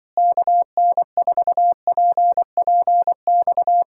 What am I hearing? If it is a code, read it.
Morse code, heard as a keyed tone, about 24 wpm, 700 Hz. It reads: KN4PPX
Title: Speed-Racing Common QSO Phrases - 20wpm